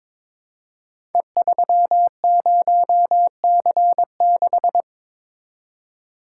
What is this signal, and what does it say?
Morse code, heard as a keyed tone, about 22 wpm, 700 Hz. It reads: E30C6
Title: Code Groups - 5 Letter:Numbers - 22wpm